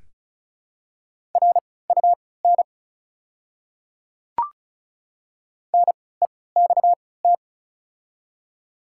Morse code, heard as a keyed tone, about 36 wpm, spaced as for 20 wpm, 700 Hz. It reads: RUN NEXT
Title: 3x Character Spacing - Top 300 Words - 35wpm